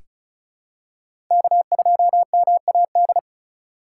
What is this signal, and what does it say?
Morse code, heard as a keyed tone, about 36 wpm, 690 Hz. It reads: K2MAD
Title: USA Call Signs - 5 Characters - 3x Character Spacing - 35wpm